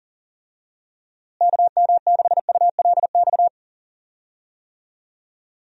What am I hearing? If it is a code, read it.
Morse code, heard as a keyed tone, about 40 wpm, 700 Hz. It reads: KM6ULX
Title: USA Call Signs - 6 Characters - Repeated 2x - 40wpm